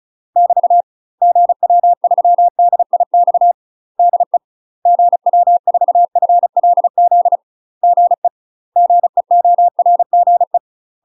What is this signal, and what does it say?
Morse code, heard as a keyed tone, about 35 wpm, 700 Hz. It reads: <BT> GW3DIX DE GW4FLZ GE GEORGE
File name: UK MASTER QSOs TRAINING / 35WPM / UK MASTER QSOs 0015